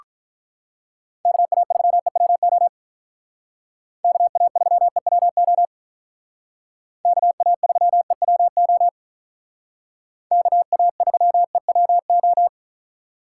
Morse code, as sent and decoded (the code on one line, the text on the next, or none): -.- .- ...-- . .-- --- / -.- .- ...-- . .-- --- / -.- .- ...-- . .-- --- / -.- .- ...-- . .-- ---
KA3EWO KA3EWO KA3EWO KA3EWO